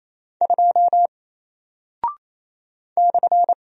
Morse code, heard as a keyed tone, about 28 wpm, 700 Hz. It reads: 2 /
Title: Single Character (Letter:Number:Punctuation:Prosign) Alt 28wpm